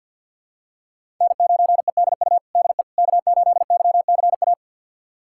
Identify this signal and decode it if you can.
Morse code, heard as a keyed tone, about 46 wpm, 700 Hz. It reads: N9EDU DE K8YCA
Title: USA Callsign DE Callsign - 5 Characters - 50wpm